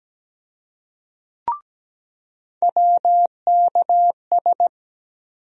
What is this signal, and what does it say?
Morse code, heard as a keyed tone, about 17 wpm, 700 Hz. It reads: WKS
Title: USA Call Sign Postfixes - 3 Letters 17wpm